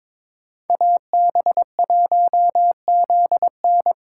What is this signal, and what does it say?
Morse code, heard as a keyed tone, about 26 wpm, 700 Hz. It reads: AB1ZN
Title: Speed Racing - USA 2x2 Call Signs 22wpm